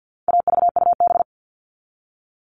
Morse code, heard as a keyed tone, about 50 wpm, 700 Hz. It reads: AVUB
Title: All Cut Numbers - 4-Digit 50wpm